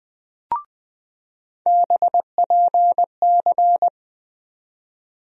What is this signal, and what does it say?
Morse code, heard as a keyed tone, about 21 wpm, 700 Hz. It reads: BPC
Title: USA Call Sign Postfixes - 3 Letters 20wpm